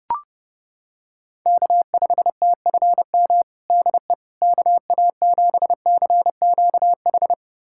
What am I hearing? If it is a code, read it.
Morse code, heard as a keyed tone, about 30 wpm, 700 Hz. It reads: K5TFM DE KA7CQH
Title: USA Callsign DE Callsign 30wpm